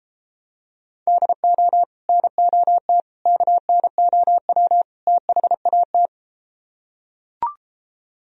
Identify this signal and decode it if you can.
Morse code, heard as a keyed tone, about 33 wpm, 700 Hz. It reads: DO NOT KNOW THAT